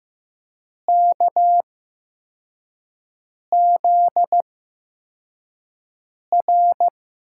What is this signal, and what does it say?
Morse code, heard as a keyed tone, about 15 wpm, spaced as for 4 wpm, 700 Hz. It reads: KZR
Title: USA Call Signs - 6 Characters - 8x Character Spacing - 15wpm